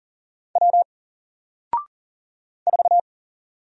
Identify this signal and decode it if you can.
Morse code, heard as a keyed tone, about 38 wpm, 700 Hz. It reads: W 4
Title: Single Character (Letter:Number:Punctuation:Prosign) 40wpm